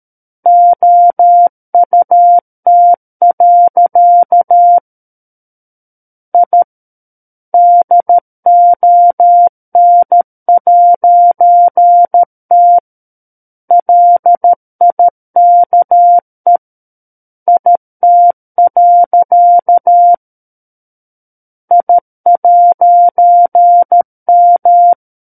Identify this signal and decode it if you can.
Morse code, heard as a keyed tone, about 13 wpm, 700 Hz. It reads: OUT. I DON'T LIKE IT. I'M
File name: Book One / 04-Chapter 4 - 13 WPM